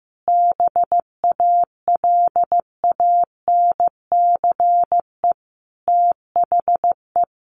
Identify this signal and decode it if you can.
Morse code, heard as a keyed tone, about 15 wpm, 700 Hz. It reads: BALANCE THE
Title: Sentences from Top 1200 Words 15wpm